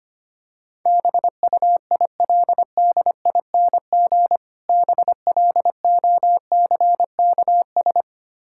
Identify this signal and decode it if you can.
Morse code, heard as a keyed tone, about 25 wpm, 700 Hz. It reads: BUILDING BLOCKS
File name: practice / 25wpm / Difficult Words with Hints - 25wpm